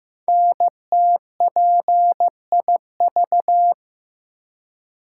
Morse code, heard as a keyed tone, about 15 wpm, 700 Hz. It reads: NTPIV
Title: Code Groups - 5 Letter:Numbers - 15wpm